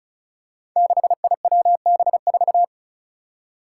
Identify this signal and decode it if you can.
Morse code, heard as a keyed tone, about 36 wpm, 700 Hz. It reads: 6IWB4
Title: Five Challenging Letter-Number Combinations 35wpm